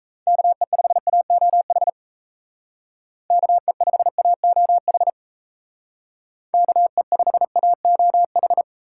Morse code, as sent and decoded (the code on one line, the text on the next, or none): -.- . ..... .- --- .... / -.- . ..... .- --- .... / -.- . ..... .- --- ....
KE5AOH KE5AOH KE5AOH